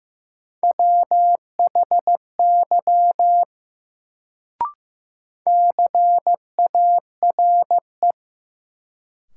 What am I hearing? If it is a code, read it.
Morse code, heard as a keyed tone, about 15 wpm, 700 Hz. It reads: WHY CARE
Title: Sets of 3 Words - New Word Drill - 15wpm